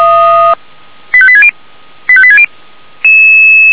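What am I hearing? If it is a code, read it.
Five-tone selective call (ZVEI), heard as quick runs of five tones, tones of about 70 ms. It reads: 85860 85860
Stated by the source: ZVEI five-tone call 85725 received on an SDR, channel 464.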